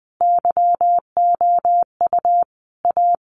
Morse code, heard as a keyed tone, about 20 wpm, 700 Hz. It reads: YOU A
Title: Sentences from Top 1100 Words 20wpm